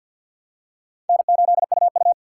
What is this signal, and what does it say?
Morse code, heard as a keyed tone, about 52 wpm, 700 Hz. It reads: N8UU
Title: USA Call Signs - 4 Characters - 2x Character Spacing - 50wpm